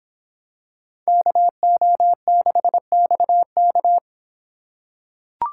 Morse code, heard as a keyed tone, about 26 wpm, 700 Hz. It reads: KO6XK